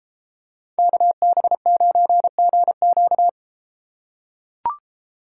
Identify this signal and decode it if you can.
Morse code, heard as a keyed tone, about 31 wpm, 700 Hz. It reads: KB9GQ